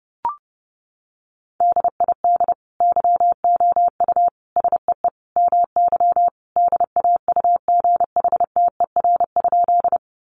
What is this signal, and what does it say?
Morse code, heard as a keyed tone, about 30 wpm, 700 Hz. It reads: DID YOU SEE MY DAUGHTER?
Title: Sentences from Top 1500 Words 30wpm